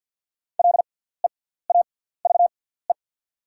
Morse code, heard as a keyed tone, about 48 wpm, spaced as for 17 wpm, 690 Hz. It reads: LEAVE
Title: 6x Character Spacing - Top 500 Words - 50wpm